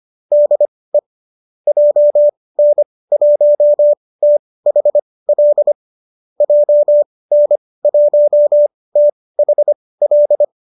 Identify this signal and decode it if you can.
Morse code, heard as a keyed tone, about 25 wpm, spaced as for 18 wpm, 590 Hz. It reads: DE JN1THL JN1THL
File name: qso205 18